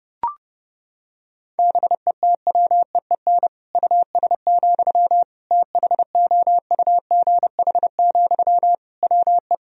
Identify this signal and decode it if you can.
Morse code, heard as a keyed tone, about 32 wpm, 700 Hz. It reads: BETWEEN US, THOUGH, WE
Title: Sentences from Top 1000 Words - Review Part 2 - 30wpm